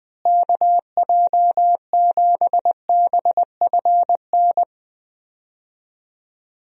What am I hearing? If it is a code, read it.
Morse code, heard as a keyed tone, about 21 wpm, 700 Hz. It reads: KJ7BFN
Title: USA Call Signs - 6 Characters - Repeated 3x - 20wpm